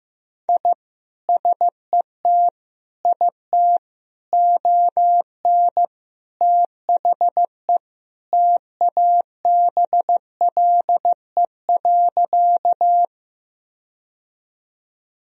Morse code, heard as a keyed tone, about 15 wpm, 700 Hz. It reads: I SET IT ON THE TABLE.
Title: Sentences from Top 1000 Words - Review Part 2 - 15wpm